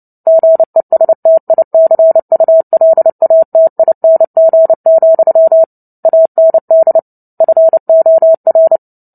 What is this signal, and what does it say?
Morse code, heard as a keyed tone, about 30 wpm, 650 Hz. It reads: GESTICULATING, AND FOR
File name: Book Two / 03-Chapter 3 - 29 WPM